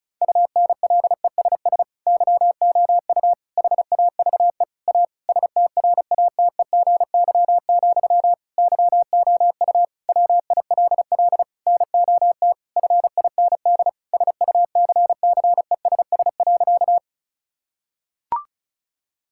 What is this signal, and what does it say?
Morse code, heard as a keyed tone, about 36 wpm, 700 Hz. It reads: UNLESS YOU HAVE A STRATEGY, YOU WILL NOT FIND SUCCESS.
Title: Sentences from Top 1000 Words - Review Part 5 - 35wpm